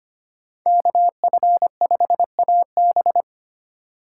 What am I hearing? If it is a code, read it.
Morse code, heard as a keyed tone, about 25 wpm, 700 Hz. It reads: KF5AB
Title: USA Call Signs - 2x Character Spacing - 25wpm